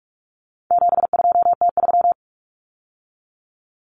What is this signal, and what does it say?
Morse code, heard as a keyed tone, about 44 wpm, 700 Hz. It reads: 72T3
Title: Common Cut Numbers - 4-Digit 45wpm